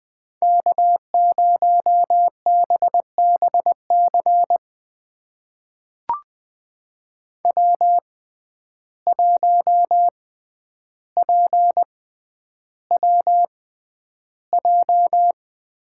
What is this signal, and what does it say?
Morse code, heard as a keyed tone, about 20 wpm, spaced as for 10 wpm, 700 Hz. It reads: K0BBC W1PWJ
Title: USA Call Signs - 5 Characters - 6x Character Spacing - 20wpm